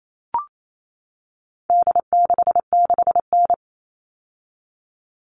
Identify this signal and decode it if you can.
Morse code, heard as a keyed tone, about 28 wpm, 700 Hz. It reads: D66N
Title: All Cut Numbers - 4-Digit 28wpm